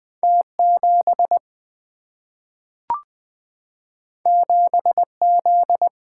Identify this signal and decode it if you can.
Morse code, heard as a keyed tone, about 20 wpm, 700 Hz. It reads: T 7 7 Z
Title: Single-Letter Single-Number 20wpm